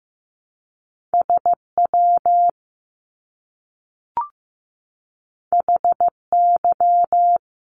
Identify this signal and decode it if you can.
Morse code, heard as a keyed tone, about 15 wpm, 700 Hz. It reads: SW HY
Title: Top Two Letter Combinations in Words 15wpm